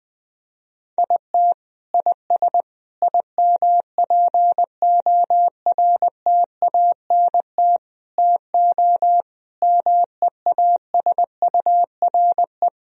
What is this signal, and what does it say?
Morse code, heard as a keyed tone, about 20 wpm, 700 Hz. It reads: IT IS IMPORTANT TO MEASURE
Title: Sentences from Top 1000 Words - Review Part 2 - 20wpm